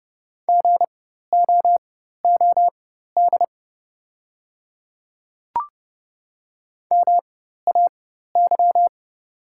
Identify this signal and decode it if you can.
Morse code, heard as a keyed tone, about 30 wpm, spaced as for 14 wpm, 700 Hz. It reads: GOOD MAY
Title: 4x Character Spacing - Top 200 Words - 30wpm